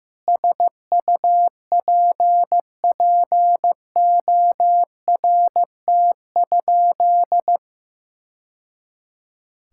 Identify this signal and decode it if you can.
Morse code, heard as a keyed tone, about 16 wpm, 700 Hz. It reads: SUPPORT?
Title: Sentences from Top 1000 Words - Review Part 5 - 15wpm